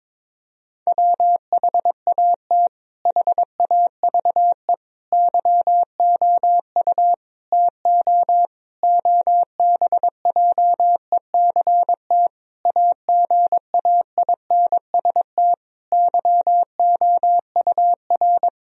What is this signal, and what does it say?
Morse code, heard as a keyed tone, about 22 wpm, 700 Hz. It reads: WHAT HAVE YOU TO OBJECT AGAINST YOUR
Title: Sentences from Top 1000 Words - Review Part 3 - 22wpm